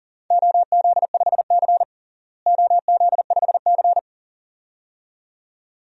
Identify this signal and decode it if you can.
Morse code, heard as a keyed tone, about 40 wpm, 700 Hz. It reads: OZ5C OZ5C
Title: DXCC-100 Call Signs - 4 Characters - Repeated 2x - 40wpm